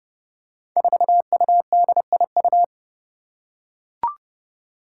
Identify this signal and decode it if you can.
Morse code, heard as a keyed tone, about 30 wpm, 700 Hz. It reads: 4UDIU
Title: Five Challenging Letter-Number Combinations 30wpm